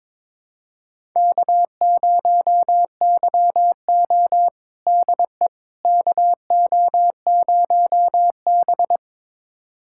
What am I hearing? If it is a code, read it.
Morse code, heard as a keyed tone, about 21 wpm, 700 Hz. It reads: K0YO DE KO0B
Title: USA Callsign DE Callsign - 4 Characters - 22wpm